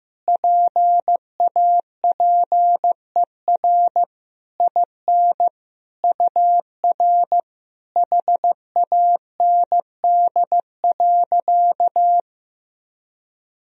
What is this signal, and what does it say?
Morse code, heard as a keyed tone, about 15 wpm, 700 Hz. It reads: PAPER IN UR HAND.
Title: Sentences from Top 400 Words and Abbreviations 15wpm